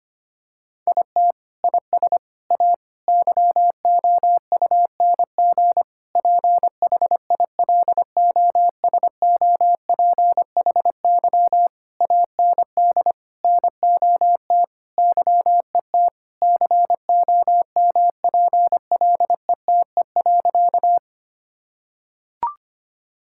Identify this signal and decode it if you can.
Morse code, heard as a keyed tone, about 25 wpm, 700 Hz. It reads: IT IS A YOUNG PHILOSOPHY AND NOT YET COMPLETE.
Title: Sentences from Top 1000 Words - Review Part 6 - 25wpm